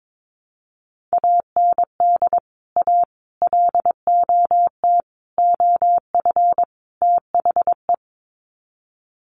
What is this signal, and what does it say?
Morse code, heard as a keyed tone, about 22 wpm, 700 Hz. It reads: AND A LOT OF THE
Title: Sets of 5 Words Encore 22wpm